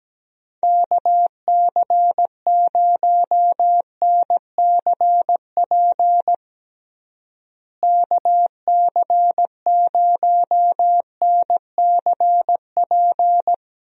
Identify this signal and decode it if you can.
Morse code, heard as a keyed tone, about 17 wpm, 700 Hz. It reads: KC0NCP KC0NCP
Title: USA Call Signs - 6 Characters - Repeated 3x - 17wpm